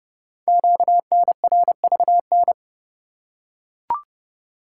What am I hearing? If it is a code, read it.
Morse code, heard as a keyed tone, about 29 wpm, 700 Hz. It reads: QNRVN
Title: Code Groups - 5 Letters - 30wpm